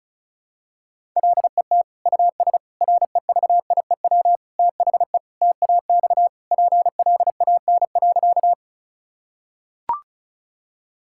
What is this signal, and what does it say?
Morse code, heard as a keyed tone, about 36 wpm, 700 Hz. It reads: LET US REVIEW THE TAX PLAN.
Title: Sentences from Top 1000 Words - Review Part 6 - 35wpm